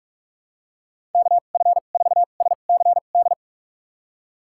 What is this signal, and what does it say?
Morse code, heard as a keyed tone, about 41 wpm, spaced as for 31 wpm, 700 Hz. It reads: KF4SCD
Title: USA Call Signs - 6 Characters - 2x Character Spacing - 45wpm